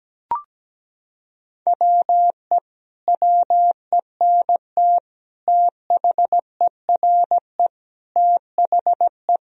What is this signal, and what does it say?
Morse code, heard as a keyed tone, about 17 wpm, 700 Hz. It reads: WE WENT THERE THE
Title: Sentences from Top 1000 Words - Review Part 4 - 17wpm